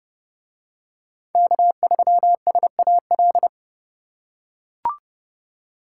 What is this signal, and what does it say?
Morse code, heard as a keyed tone, about 28 wpm, 700 Hz. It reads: K3SAL